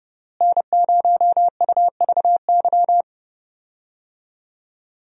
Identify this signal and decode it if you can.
Morse code, heard as a keyed tone, about 30 wpm, 700 Hz. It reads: N0UVY
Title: USA Call Signs - 5 Characters - Repeated 2x - 30wpm